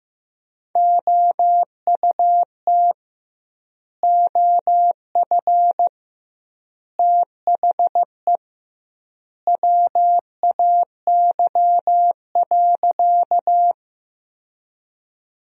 Morse code, heard as a keyed tone, about 15 wpm, 700 Hz. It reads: OUT OF THE WAY.
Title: 2x Word-Spacing - Sentences from Top 100 Words - 15wpm